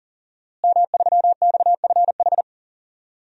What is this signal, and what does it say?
Morse code, heard as a keyed tone, about 37 wpm, 700 Hz. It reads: M3XFH
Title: DXCC-100 Call Signs - 5 Characters - Repeated 2x - 40wpm